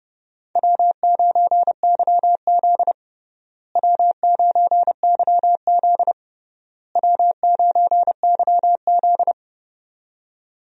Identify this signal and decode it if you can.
Morse code, heard as a keyed tone, about 30 wpm, 700 Hz. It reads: W9YZ W9YZ W9YZ
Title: USA Call Signs - 4 Characters - Repeated 3x - 30wpm